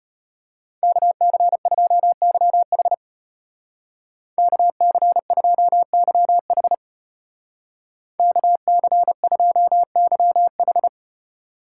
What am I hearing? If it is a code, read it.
Morse code, heard as a keyed tone, about 30 wpm, 700 Hz. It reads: KC2YH KC2YH KC2YH
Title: Speed Racing - USA 2x2 Call Signs 25wpm